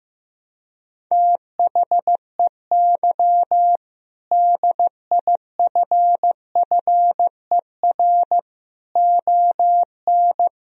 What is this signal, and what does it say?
Morse code, heard as a keyed tone, about 16 wpm, 700 Hz. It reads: THEY DIFFER ON
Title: Sentences from Top 1000 Words - Review Part 1 - 15wpm